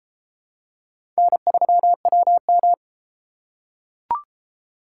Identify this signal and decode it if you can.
Morse code, heard as a keyed tone, about 30 wpm, 700 Hz. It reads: N3WM